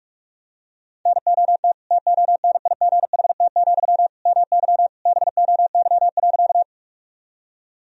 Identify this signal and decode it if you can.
Morse code, heard as a keyed tone, about 43 wpm, 700 Hz. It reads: NOT TONIGHT, MY BOY.
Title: Sentences from Top 1000 Words - Review Part 3 - 45wpm